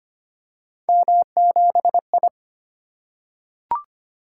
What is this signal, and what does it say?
Morse code, heard as a keyed tone, about 25 wpm, 700 Hz. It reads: M7I
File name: Three Letter-Number Combinations - 25wpm